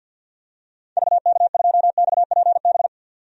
Morse code, heard as a keyed tone, about 52 wpm, 700 Hz. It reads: VK2XPB
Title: DXCC-100 Call Signs - 6 Characters - 50wpm